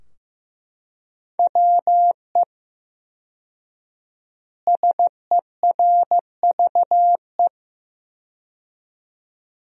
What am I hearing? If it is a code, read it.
Morse code, heard as a keyed tone, about 15 wpm, 700 Hz. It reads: WE SERVE
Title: 4x Word-Spacing - Sentences from Top 400 Words - 15wpm